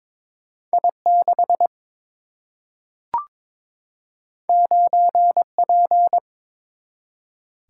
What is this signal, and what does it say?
Morse code, heard as a keyed tone, about 21 wpm, 700 Hz. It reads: I 6 9 P
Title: Single-Letter Single-Number 22wpm